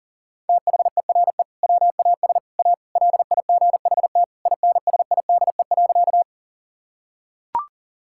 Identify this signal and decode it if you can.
Morse code, heard as a keyed tone, about 40 wpm, 700 Hz. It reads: THERE WAS A LIGHT INSIDE.
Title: Sentences from Top 1000 Words - Review Part 6 - 40wpm